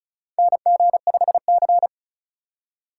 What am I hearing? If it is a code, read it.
Morse code, heard as a keyed tone, about 34 wpm, 700 Hz. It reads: NG5C
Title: USA Call Signs - 4 Characters - 3x Character Spacing - 35wpm